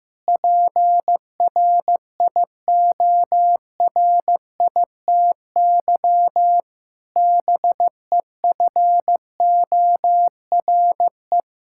Morse code, heard as a keyed tone, about 15 wpm, 700 Hz. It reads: PRIORITY BEFORE
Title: Sentences from Top 1600 Words 15wpm